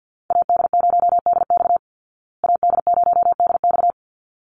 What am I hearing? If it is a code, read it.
Morse code, heard as a keyed tone, about 48 wpm, 700 Hz. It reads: AD0DX AD0DX
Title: Call Signs Repeated 3x - 50wpm